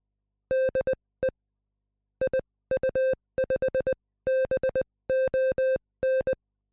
Morse code, heard as a keyed tone, about 20 wpm, spaced as for 16 wpm, 520 Hz. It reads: DE IU5BON